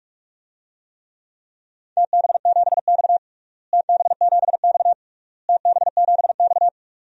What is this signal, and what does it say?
Morse code, heard as a keyed tone, about 45 wpm, 690 Hz. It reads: TB7X TB7X TB7X
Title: DXCC-100 Call Signs - 4 Characters - Repeated 3x - 45wpm